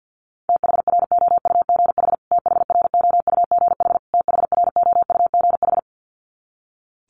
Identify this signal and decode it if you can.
Morse code, heard as a keyed tone, about 48 wpm, 700 Hz. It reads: THROUGH THROUGH THROUGH
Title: Top 200 Words Repeated 50wpm